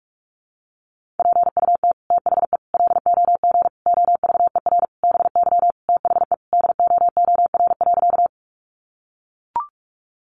Morse code, heard as a keyed tone, about 44 wpm, 700 Hz. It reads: PUT THE LOG OVER BY THE DOOR.
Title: Sentences from Top 1400 Words 45wpm